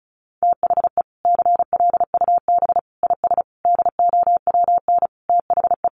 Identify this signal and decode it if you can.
Morse code, heard as a keyed tone, about 34 wpm, 700 Hz. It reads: THE CLUB IS DOWN THE
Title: Sentences from Top 1200 Words 35wpm